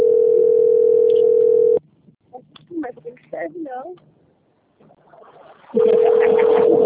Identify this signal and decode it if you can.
Ringback tone (North America), 440 + 480 Hz, with speech.
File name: rg-1600-unknown- - - .54028